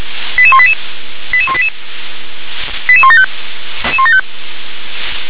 Five-tone selective call (ZVEI), heard as quick runs of five tones, tones of about 70 ms. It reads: 88188 88188 88175